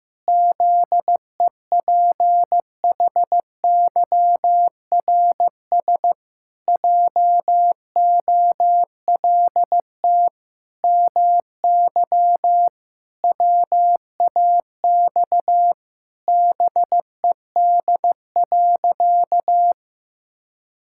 Morse code, heard as a keyed tone, about 14 wpm, 700 Hz. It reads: ZEPHYRS JOLT MY WAX BED.
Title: Pangram Challenge v2 - 15wpm